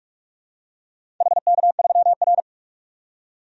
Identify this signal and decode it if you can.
Morse code, heard as a keyed tone, about 48 wpm, 700 Hz. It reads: HK3R
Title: DXCC-100 Call Signs - 4 Characters - 45wpm